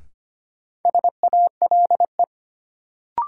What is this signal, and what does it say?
Morse code, heard as a keyed tone, about 25 wpm, 700 Hz. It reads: SALE
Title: Sentences from Top 1400 Words - New Word Drill - 25wpm